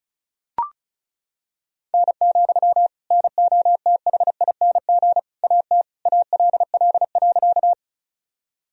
Morse code, heard as a keyed tone, about 34 wpm, 700 Hz. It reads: N, NOTHING AT ALL.
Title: Sentences from Top 500 Words and Abbreviations 35wpm